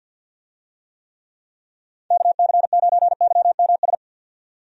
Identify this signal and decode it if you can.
Morse code, heard as a keyed tone, about 52 wpm, 700 Hz. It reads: KC9YMS
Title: USA Call Signs - 6 Characters - 50wpm